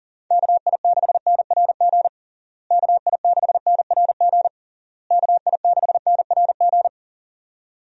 Morse code, heard as a keyed tone, about 40 wpm, 700 Hz. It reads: KI6NRG KI6NRG KI6NRG
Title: USA Call Signs - 6 Characters - Repeated 3x - 40wpm